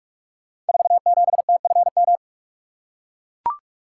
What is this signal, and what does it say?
Morse code, heard as a keyed tone, about 44 wpm, 700 Hz. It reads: 47TFM
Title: Code Groups - 5 Letter:Numbers - 45wpm